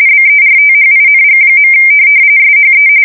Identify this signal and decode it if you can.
RTTY, about 45.45 baud. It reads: BERTHING. SHE
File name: fsk170 / USS-Missouri